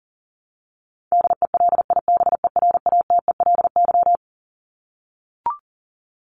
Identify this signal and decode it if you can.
Morse code, heard as a keyed tone, about 40 wpm, 700 Hz. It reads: DELIBERATELY
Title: Top 1751-2000 Words 40wpm